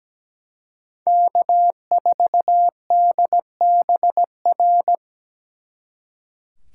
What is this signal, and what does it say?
Morse code, heard as a keyed tone, about 20 wpm, 700 Hz. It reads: K4DBR
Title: Speed Racing - USA 1x3 Call Signs 17wpm